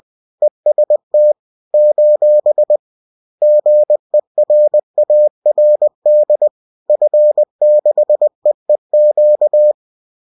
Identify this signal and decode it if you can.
Morse code, heard as a keyed tone, about 20 wpm, 600 Hz. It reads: EST : GERARD F6EEQ